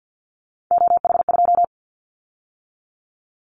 Morse code, heard as a keyed tone, about 50 wpm, 700 Hz. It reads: OH2
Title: Callsign Prefixes for Top 50 DXCC - 50wpm